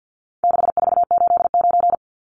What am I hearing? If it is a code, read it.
Morse code, heard as a keyed tone, about 50 wpm, 700 Hz. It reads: 6489
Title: Numbers - 4-Digit 50wpm